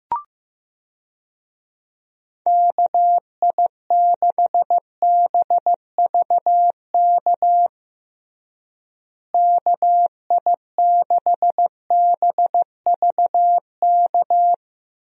Morse code, heard as a keyed tone, about 16 wpm, 700 Hz. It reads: KI6BVK KI6BVK
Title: USA Call Signs - 6 Characters - Repeated 3x - 15wpm